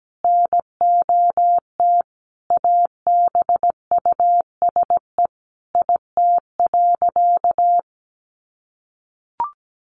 Morse code, heard as a keyed tone, about 17 wpm, 700 Hz. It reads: NOT ABUSE IT.
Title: Sentences from Top 1300 Words 17wpm